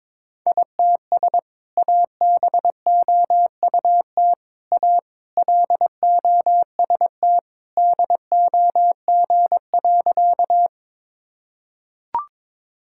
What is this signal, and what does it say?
Morse code, heard as a keyed tone, about 22 wpm, 700 Hz. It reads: ITS ABOUT A LOST DOG.
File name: practice / 22wpm / Sentences with 2-Word Phrases - 22wpm